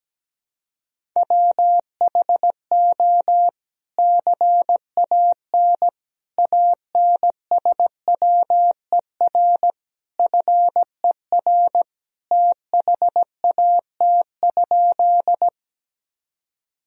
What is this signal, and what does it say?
Morse code, heard as a keyed tone, about 17 wpm, 700 Hz. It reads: WHO CAN ANSWER FER THAT?
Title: Sentences from Top 300 Words and Abbreviations 17wpm